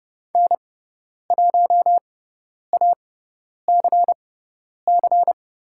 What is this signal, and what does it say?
Morse code, heard as a keyed tone, about 30 wpm, spaced as for 10 wpm, 700 Hz. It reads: N1ACC
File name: Lesson 17g - Letter u - Call Signs 30wpm